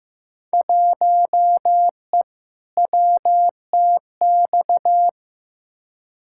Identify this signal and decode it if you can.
Morse code, heard as a keyed tone, about 15 wpm, 700 Hz. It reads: 1E WTX
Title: Field Day Exchanges - 15wpm